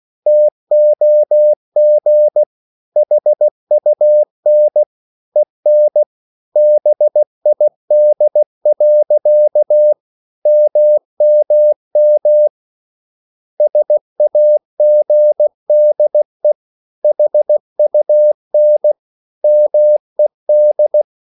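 Morse code, heard as a keyed tone, about 16 wpm, 600 Hz. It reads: TOG HUN EN BID. MMM SAGDE HUN MED
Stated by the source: Slow (QRS) Morse code at 16 wpm, from CW med Gnister.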